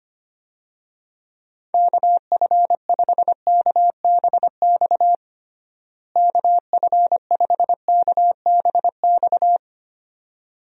Morse code, heard as a keyed tone, about 25 wpm, 700 Hz. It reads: KF5KBX KF5KBX
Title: USA Call Signs - 6 Characters - Repeated 2x - 25wpm